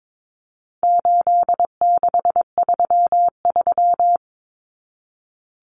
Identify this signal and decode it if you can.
Morse code, heard as a keyed tone, about 22 wpm, 700 Hz. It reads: 8633
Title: Numbers - 4-Digit 22wpm